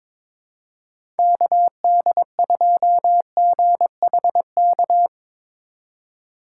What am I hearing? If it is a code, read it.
Morse code, heard as a keyed tone, about 22 wpm, 700 Hz. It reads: KD2GHK
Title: USA Call Signs - 6 Characters - 2x Character Spacing - 22wpm